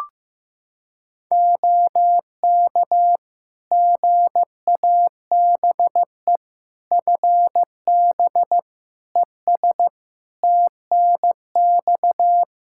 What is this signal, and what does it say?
Morse code, heard as a keyed tone, about 15 wpm, 700 Hz. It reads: OK GABE FB ES TNX
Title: Practice QSO - Exchange 2 - Info - 15wpm